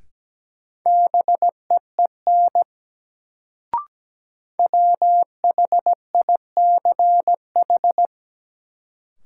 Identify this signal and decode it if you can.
Morse code, heard as a keyed tone, about 17 wpm, 700 Hz. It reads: BEEN WHICH
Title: Sentences from Top 100 Words - New Word Drill - 17wpm